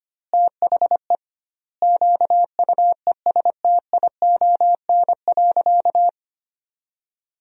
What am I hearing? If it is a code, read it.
Morse code, heard as a keyed tone, about 25 wpm, 700 Hz. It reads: THE QUESTION.
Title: 2x Word-Spacing - Sentences from Top 400 Words - 25wpm